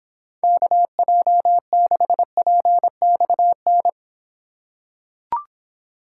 Morse code, heard as a keyed tone, about 26 wpm, 700 Hz. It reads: KJ6PXN